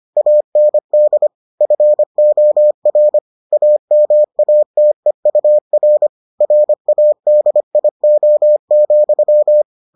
Morse code, heard as a keyed tone, about 25 wpm, 600 Hz. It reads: AND FOR AMATEUR RADIO,